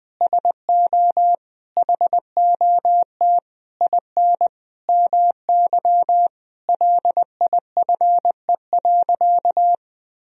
Morse code, heard as a keyed tone, about 20 wpm, 700 Hz. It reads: SO HOT IN MY LIFE.
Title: Sentences from Top 1000 Words - Review Part 3 - 20wpm